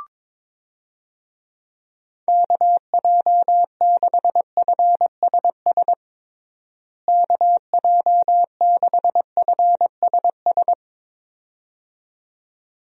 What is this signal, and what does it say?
Morse code, heard as a keyed tone, about 22 wpm, 700 Hz. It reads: KJ6FSS KJ6FSS
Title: USA Call Signs - 6 Characters - Repeated 2x - 22wpm